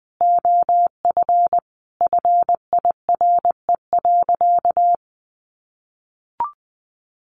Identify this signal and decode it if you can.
Morse code, heard as a keyed tone, about 20 wpm, 700 Hz. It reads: OF FIRE.
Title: Easy Phrases 20wpm